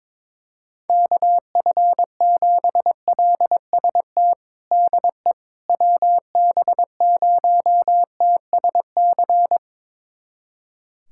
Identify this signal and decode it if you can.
Morse code, heard as a keyed tone, about 22 wpm, 700 Hz. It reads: KF7LST DE WB0TSC
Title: USA Callsign DE Callsign 22wpm